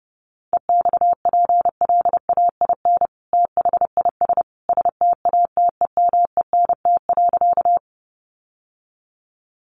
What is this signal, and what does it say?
Morse code, heard as a keyed tone, about 30 wpm, 700 Hz. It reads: EXPLAIN THIS STATEMENT.
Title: Sentences from Top 1200 Words 30wpm